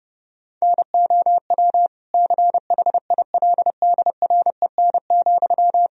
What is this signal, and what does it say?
Morse code, heard as a keyed tone, about 32 wpm, 700 Hz. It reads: NOW CHILDREN,
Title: Sentences from Top 1000 Words - Review Part 1 - 30wpm